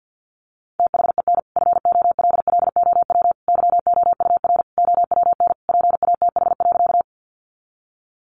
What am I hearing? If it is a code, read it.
Morse code, heard as a keyed tone, about 50 wpm, 700 Hz. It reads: THEN FOLLOW YOUR OWN PATH.
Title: Sentences from Top 1400 Words 50wpm